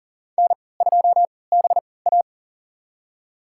Morse code, heard as a keyed tone, about 40 wpm, spaced as for 23 wpm, 700 Hz. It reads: N2BA
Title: USA Call Signs - 3x Character Spacing - 40wpm